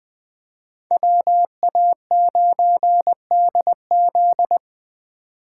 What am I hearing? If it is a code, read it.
Morse code, heard as a keyed tone, about 20 wpm, 700 Hz. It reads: WA9DZ